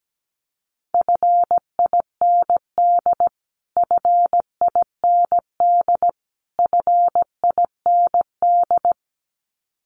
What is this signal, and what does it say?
Morse code, heard as a keyed tone, about 17 wpm, 700 Hz. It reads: FIND FIND FIND
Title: Top 500 Words Repeated 17wpm